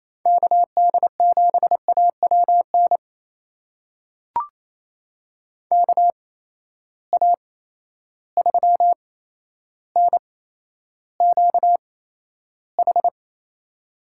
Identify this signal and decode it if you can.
Morse code, heard as a keyed tone, about 28 wpm, spaced as for 12 wpm, 700 Hz. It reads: KD7AWN KA3NQH
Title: USA Call Signs - 6 Characters - 8x Character Spacing - 28wpm